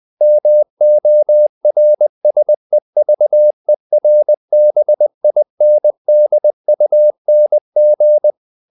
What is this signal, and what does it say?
Morse code, heard as a keyed tone, about 20 wpm, 600 Hz. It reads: MORSEVERBINDUNG
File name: dk5ke / Humor-100BpM